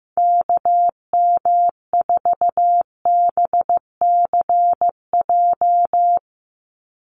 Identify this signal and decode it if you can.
Morse code, heard as a keyed tone, about 15 wpm, 700 Hz. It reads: KM4BCJ
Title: USA Call Signs 15wpm